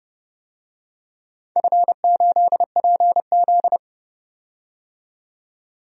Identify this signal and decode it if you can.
Morse code, heard as a keyed tone, about 32 wpm, 700 Hz. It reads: F8PZ
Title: DXCC-100 Call Signs - 4 Characters - 30wpm